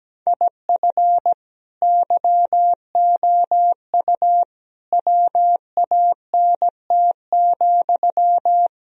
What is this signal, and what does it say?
Morse code, heard as a keyed tone, about 17 wpm, 700 Hz. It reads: IF YOU WANT,
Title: Sentences from Top 1000 Words - Review Part 3 - 17wpm